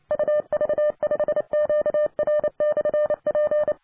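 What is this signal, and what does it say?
Morse code, heard as a keyed tone, about 27 wpm, 600 Hz. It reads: UV5QR/P
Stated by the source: 20 m band signal from RD3ARU with a Zeppelin antenna, reported 559.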